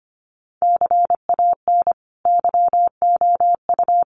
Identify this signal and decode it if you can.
Morse code, heard as a keyed tone, about 25 wpm, 700 Hz. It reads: CAN YOU
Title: Sentences from Top 1300 Words 25wpm